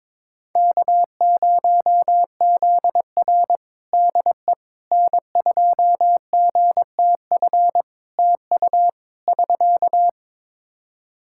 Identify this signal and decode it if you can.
Morse code, heard as a keyed tone, about 22 wpm, 700 Hz. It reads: K0ZR DE N2GTF TU <SK>
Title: Practice QSO - Exchange 4 - Ending - 22wpm